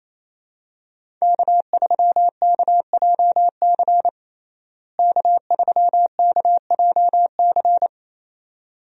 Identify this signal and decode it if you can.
Morse code, heard as a keyed tone, about 28 wpm, 700 Hz. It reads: K3KJC K3KJC
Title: USA Call Signs - 5 Characters - Repeated 2x - 28wpm